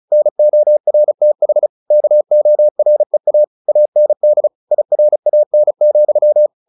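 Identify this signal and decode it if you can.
Morse code, heard as a keyed tone, about 36 wpm, 600 Hz. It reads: NORTH KOREA AND IRAN,
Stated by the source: Morse practice at 35 wpm from CW med Gnister, sending Ken Follett's novel Never in English.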